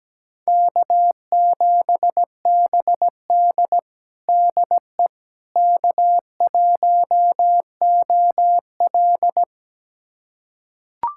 Morse code, heard as a keyed tone, about 17 wpm, 700 Hz. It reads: K7BD DE K1OL
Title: USA Callsign DE Callsign - 4 Characters - 17wpm